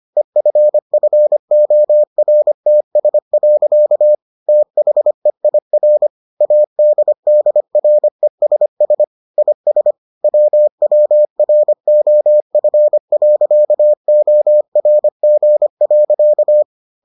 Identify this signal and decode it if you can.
Morse code, heard as a keyed tone, about 25 wpm, 600 Hz. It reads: EFFORTS. THEIR ADDRESS IS WWROF.ORG.